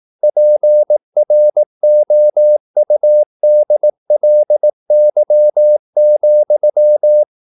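Morse code, heard as a keyed tone, about 18 wpm, 600 Hz. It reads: PROUDLY,
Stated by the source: Morse practice at 18 wpm, English text, from CW med Gnister.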